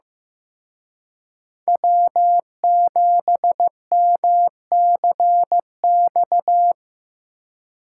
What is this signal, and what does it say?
Morse code, heard as a keyed tone, about 17 wpm, 700 Hz. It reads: W7MCX
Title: Speed Racing - USA 1x3 Call Signs 15wpm